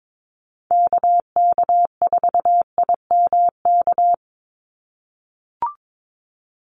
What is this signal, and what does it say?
Morse code, heard as a keyed tone, about 22 wpm, 700 Hz. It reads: KK4IMK
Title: USA Call Signs 22wpm